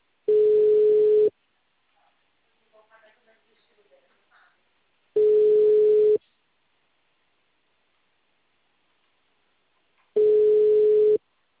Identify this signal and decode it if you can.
Ringback tone (Europe, 425 Hz), with speech.